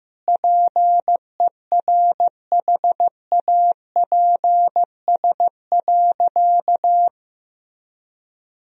Morse code, heard as a keyed tone, about 15 wpm, 700 Hz. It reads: PERHAPS.
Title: Sentences from Top 1000 Words - Review Part 1 - 15wpm